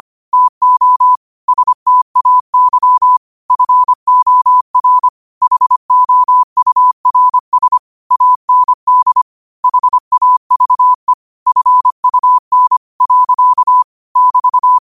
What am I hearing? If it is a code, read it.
Morse code, heard as a keyed tone, about 25 wpm, 1000 Hz. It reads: TO STAY FOR HOURS AND HAVE FUN. <BT>
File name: qotdfiles / 25wpm